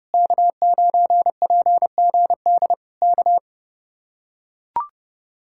Morse code, heard as a keyed tone, about 30 wpm, 700 Hz. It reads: K9PGD K